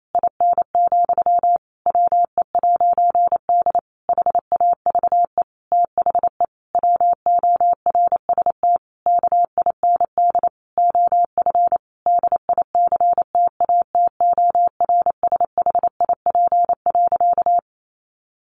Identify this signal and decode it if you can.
Morse code, heard as a keyed tone, about 28 wpm, 700 Hz. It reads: IN, WE'D HAVE THE WORST KIND OF DICTATORSHIP.
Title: A Sound Of Thunder 28wpm